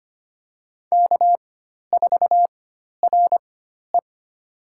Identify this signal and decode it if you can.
Morse code, heard as a keyed tone, about 25 wpm, spaced as for 12 wpm, 700 Hz. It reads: K4RE
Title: USA Call Signs - 4 Characters - 4x Character Spacing - 25wpm